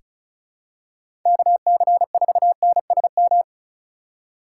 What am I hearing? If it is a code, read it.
Morse code, heard as a keyed tone, about 34 wpm, 700 Hz. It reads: KC4NSM
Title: USA Call Signs - 6 Characters - 2x Character Spacing - 35wpm